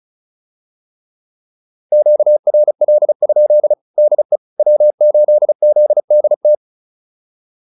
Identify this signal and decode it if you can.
Morse code, heard as a keyed tone, about 36 wpm, 600 Hz. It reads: QRL? DE W8ZDT